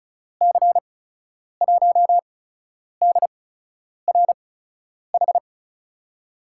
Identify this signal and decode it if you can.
Morse code, heard as a keyed tone, about 36 wpm, spaced as for 10 wpm, 700 Hz. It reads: C1DRH
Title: USA Call Signs - 6 Characters - 8x Character Spacing - 35wpm